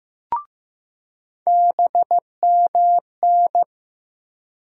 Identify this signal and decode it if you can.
Morse code, heard as a keyed tone, about 15 wpm, 700 Hz. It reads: BMN
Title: USA Call Sign Postfixes - 3 Letters 15wpm